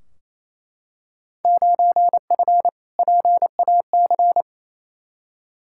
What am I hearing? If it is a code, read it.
Morse code, heard as a keyed tone, about 28 wpm, 700 Hz. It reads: 9F PAC
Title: Field Day Exchanges Encore - 28wpm